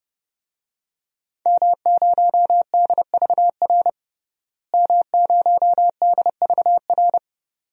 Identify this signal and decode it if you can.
Morse code, heard as a keyed tone, about 30 wpm, 700 Hz. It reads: M0DVR M0DVR
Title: DXCC-100 Call Signs - 5 Characters - Repeated 2x - 30wpm